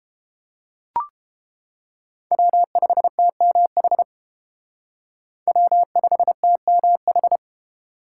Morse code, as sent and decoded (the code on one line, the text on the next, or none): .-- ..... - -- .... / .-- ..... - -- ....
W5TMH W5TMH